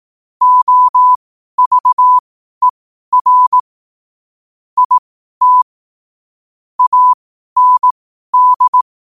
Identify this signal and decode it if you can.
Morse code, heard as a keyed tone, about 18 wpm, spaced as for 12 wpm, 1000 Hz. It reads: OVER IT AND
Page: New quotes every day in morse code at 12 Words per minute.